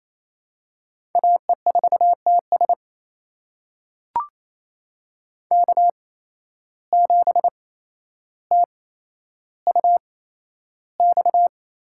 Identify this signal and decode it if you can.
Morse code, heard as a keyed tone, about 27 wpm, spaced as for 12 wpm, 700 Hz. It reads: AE4TS K7TUX
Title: USA Call Signs - 5 Characters - 8x Character Spacing - 28wpm